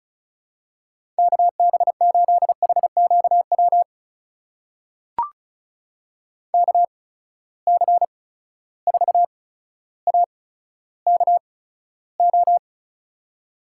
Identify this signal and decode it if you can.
Morse code, heard as a keyed tone, about 36 wpm, spaced as for 15 wpm, 700 Hz. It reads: KB8HQW KC4AKO
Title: USA Call Signs - 6 Characters - 8x Character Spacing - 35wpm